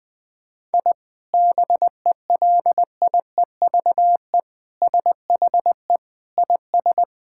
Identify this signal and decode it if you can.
Morse code, heard as a keyed tone, about 20 wpm, 700 Hz. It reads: I BELIEVE SHE IS
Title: Sentences from Top 1000 Words - Review Part 5 - 20wpm